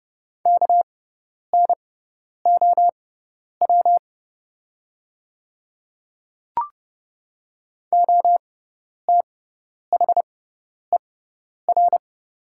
Morse code, heard as a keyed tone, about 30 wpm, spaced as for 10 wpm, 690 Hz. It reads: KNOW OTHER
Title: 6x Character Spacing - Top 200 Words - 30wpm